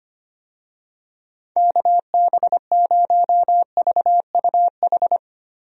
Morse code, heard as a keyed tone, about 25 wpm, 700 Hz. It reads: KB0VUH
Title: USA Call Signs - 6 Characters - 25wpm